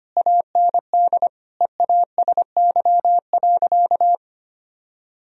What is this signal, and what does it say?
Morse code, heard as a keyed tone, about 25 wpm, 700 Hz. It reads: AND EASY.
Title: Sentences from Top 1000 Words - Review Part 3 - 25wpm